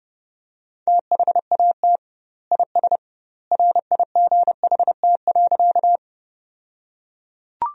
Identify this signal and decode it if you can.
Morse code, heard as a keyed tone, about 32 wpm, 700 Hz. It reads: THAT IS RIGHT.
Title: 2x Word-Spacing - Sentences from Top 200 Words - 30wpm